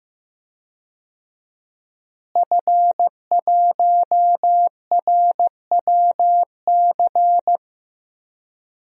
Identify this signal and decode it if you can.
Morse code, heard as a keyed tone, about 15 wpm, 700 Hz. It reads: F1RWC
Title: DXCC-100 Call Signs - 5 Characters - Repeated 3x - 15wpm